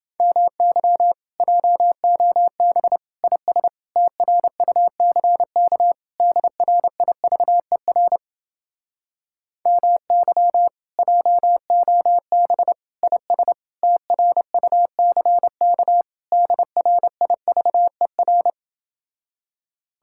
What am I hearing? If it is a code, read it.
Morse code, heard as a keyed tone, about 31 wpm, 700 Hz. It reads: MY JOB IS TRUCK DRIVER MY JOB IS TRUCK DRIVER